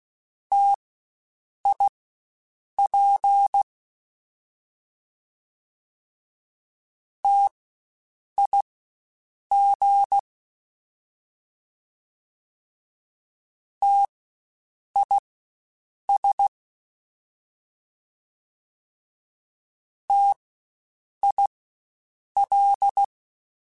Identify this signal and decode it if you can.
Morse code, heard as a keyed tone, about 16 wpm, spaced as for 7 wpm, 800 Hz. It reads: TIP TIG TIS TIL